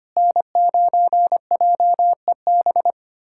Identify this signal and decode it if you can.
Morse code, heard as a keyed tone, about 25 wpm, 700 Hz. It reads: N9JEB
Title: USA Callsign DE Callsign 25wpm